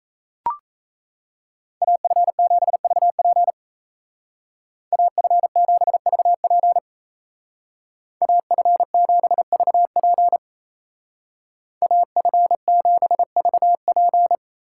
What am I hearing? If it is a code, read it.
Morse code, heard as a keyed tone, about 35 wpm, 700 Hz. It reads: AF7VP AF7VP AF7VP AF7VP
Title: Speed Racing - USA 2x2 Call Signs 28wpm